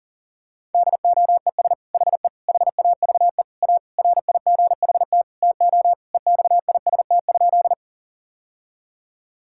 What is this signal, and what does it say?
Morse code, heard as a keyed tone, about 40 wpm, 700 Hz. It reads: DOES HE HAVE A RIGHT TO EXIST?
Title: Sentences from Top 1000 Words - Review Part 5 - 40wpm